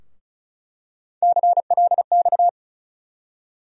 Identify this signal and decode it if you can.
Morse code, heard as a keyed tone, about 34 wpm, 700 Hz. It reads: CLX
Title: USA Call Sign Postfixes - 3 Letters 35wpm